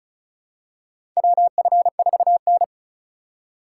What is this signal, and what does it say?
Morse code, heard as a keyed tone, about 36 wpm, 700 Hz. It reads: WF4N
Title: USA Call Signs - 4 Characters - 3x Character Spacing - 35wpm